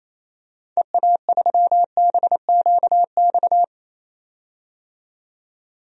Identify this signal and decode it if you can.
Morse code, heard as a keyed tone, about 28 wpm, 700 Hz. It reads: EA3BQX
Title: DXCC-100 Call Signs - 6 Characters - 28wpm